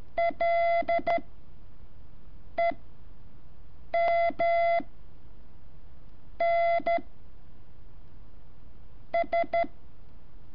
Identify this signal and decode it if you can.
Morse code, heard as a keyed tone, about 12 wpm, 680 Hz. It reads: L E M N S
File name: curso / ejercicio 16